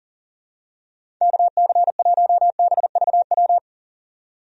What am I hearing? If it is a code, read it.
Morse code, heard as a keyed tone, about 40 wpm, 700 Hz. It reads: KC1BVW
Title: USA Call Signs - 6 Characters - 40wpm